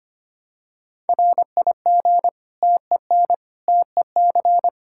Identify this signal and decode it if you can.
Morse code, heard as a keyed tone, about 25 wpm, 700 Hz. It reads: RIG TEN TEC
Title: Practice QSO - Exchange 2 - Info - 25wpm